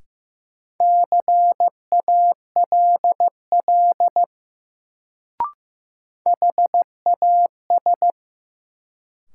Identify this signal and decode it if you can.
Morse code, heard as a keyed tone, about 15 wpm, 700 Hz. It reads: CALL HAS
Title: Sentences from Top 100 Words - New Word Drill - 15wpm